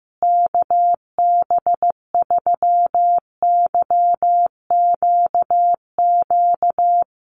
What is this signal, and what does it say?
Morse code, heard as a keyed tone, about 15 wpm, 700 Hz. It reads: KB3YQQ
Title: USA Call Signs Repeated 3x 15wpm